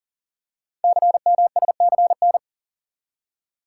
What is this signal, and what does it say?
Morse code, heard as a keyed tone, about 44 wpm, 700 Hz. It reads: CMSCN
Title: Code Groups - 5 Letters - 40wpm